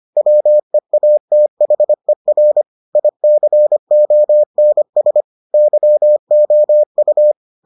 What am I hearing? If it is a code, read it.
Morse code, heard as a keyed tone, about 25 wpm, 600 Hz. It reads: WEATHER ICONS YOU